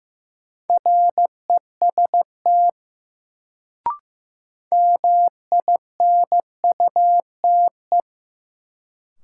Morse code, heard as a keyed tone, about 15 wpm, 700 Hz. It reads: REST MINUTE
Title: Sentences from Top 500 Words - New Word Drill - 15wpm